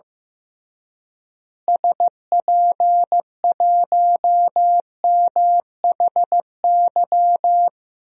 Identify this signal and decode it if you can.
Morse code, heard as a keyed tone, about 16 wpm, 700 Hz. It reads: SP1MHY
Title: DXCC-100 Call Signs - 6 Characters - Repeated 3x - 15wpm